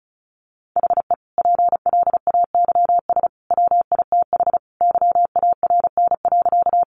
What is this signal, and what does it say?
Morse code, heard as a keyed tone, about 34 wpm, 700 Hz. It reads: HE PLAYS WITH YARN.
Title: Easy Phrases 35wpm